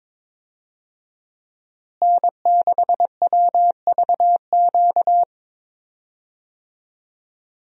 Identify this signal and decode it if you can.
Morse code, heard as a keyed tone, about 21 wpm, 700 Hz. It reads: N6WVQ
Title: USA Call Signs - 5 Characters - Repeated 3x - 22wpm